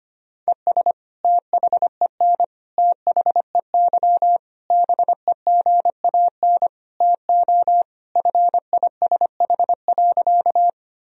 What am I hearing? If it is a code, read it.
Morse code, heard as a keyed tone, about 25 wpm, 700 Hz. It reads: ES THEN THEY BEGAN TO FISH.
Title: Sentences from Top 300 Words and Abbreviations 25wpm